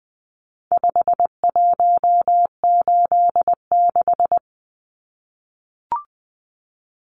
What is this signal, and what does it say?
Morse code, heard as a keyed tone, about 20 wpm, 700 Hz. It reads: 5186
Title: Numbers - 4-Digit 20wpm